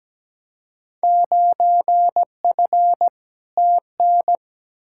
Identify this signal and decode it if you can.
Morse code, heard as a keyed tone, about 17 wpm, 700 Hz. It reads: 9F TN
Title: Field Day Exchanges Encore - 17wpm